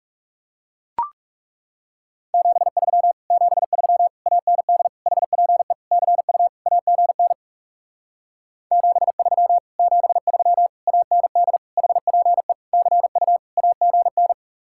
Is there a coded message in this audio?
Yes: Morse code from 2.3 to 14.3 s, heard as a keyed tone, about 40 wpm, 700 Hz. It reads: 73 73 AND HPE CU AGN 73 73 AND HPE CU AGN